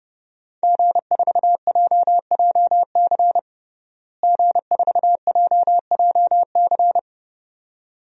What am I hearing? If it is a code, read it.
Morse code, heard as a keyed tone, about 30 wpm, 700 Hz. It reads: G4JJC G4JJC
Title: DXCC-100 Call Signs - 5 Characters - Repeated 2x - 30wpm